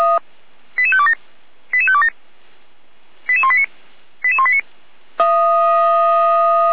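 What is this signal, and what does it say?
Five-tone selective call (ZVEI), heard as quick runs of five tones, tones of about 70 ms. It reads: 88527 88527 88189 88189
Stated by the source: ZVEI five-tone selective call, code 88191, received with an SDR.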